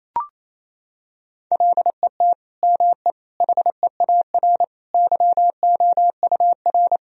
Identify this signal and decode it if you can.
Morse code, heard as a keyed tone, about 28 wpm, 700 Hz. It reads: LET ME HEAR YOUR
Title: Sentences from Top 1000 Words - Review Part 1 - 28wpm